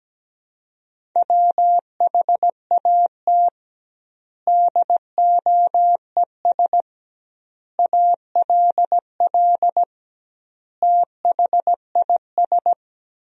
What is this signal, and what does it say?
Morse code, heard as a keyed tone, about 17 wpm, 700 Hz. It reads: WHAT DOES ALL THIS
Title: 2x Word-Spacing - Sentences from Top 200 Words - 17wpm